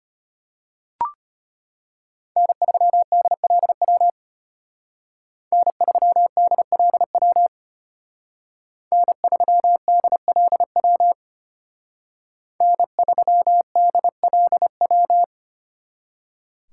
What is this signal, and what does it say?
Morse code, heard as a keyed tone, about 32 wpm, 700 Hz. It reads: N3DLW N3DLW N3DLW N3DLW